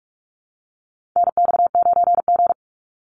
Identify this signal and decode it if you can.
Morse code, heard as a keyed tone, about 44 wpm, 700 Hz. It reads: NX9G
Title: Call Signs Repeated 3x - 45wpm